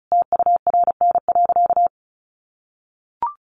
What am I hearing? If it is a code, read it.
Morse code, heard as a keyed tone, about 34 wpm, 700 Hz. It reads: TURN.
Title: Sentences from Top 600 Words 35wpm